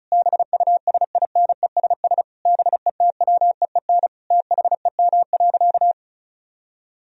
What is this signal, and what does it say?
Morse code, heard as a keyed tone, about 34 wpm, 700 Hz. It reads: BUSINESS BETWEEN THEM.
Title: Sentences from Top 1000 Words - Review Part 1 - 35wpm